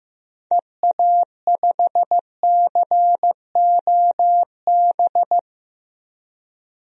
Morse code, heard as a keyed tone, about 16 wpm, 700 Hz. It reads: EA5COB
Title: DXCC-100 Call Signs - 6 Characters - Repeated 3x - 15wpm